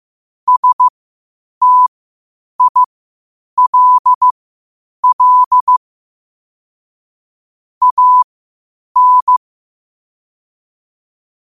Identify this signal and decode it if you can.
Morse code, heard as a keyed tone, about 15 wpm, spaced as for 8 wpm, 1000 Hz. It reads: STILL AN
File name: qotdfiles / 8wpm